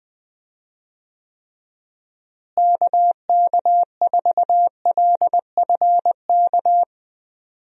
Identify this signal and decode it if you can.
Morse code, heard as a keyed tone, about 20 wpm, 700 Hz. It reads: KK4LFK
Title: USA Call Signs - 6 Characters - Repeated 3x - 20wpm